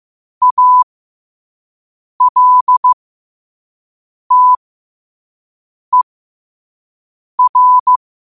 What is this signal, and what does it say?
Morse code, heard as a keyed tone, about 15 wpm, spaced as for 5 wpm, 1000 Hz. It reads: ALTER
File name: 5wpm